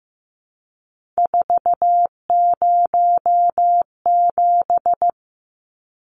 Morse code, heard as a keyed tone, about 15 wpm, 700 Hz. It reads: 407
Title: Numbers - 3-Digit 15wpm